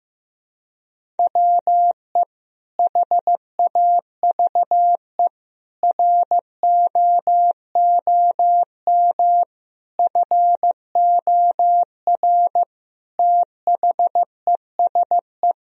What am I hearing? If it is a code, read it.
Morse code, heard as a keyed tone, about 16 wpm, 700 Hz. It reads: WE HAVE ROOM FOR THESE
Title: Sentences from Top 1000 Words - Review Part 1 - 15wpm